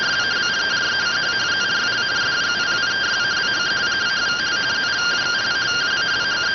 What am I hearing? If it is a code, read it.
Two-tone teleprinter signal, about 170 Hz shift.